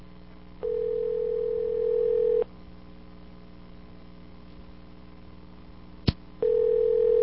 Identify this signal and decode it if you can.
Ringback tone (North America), 440 + 480 Hz.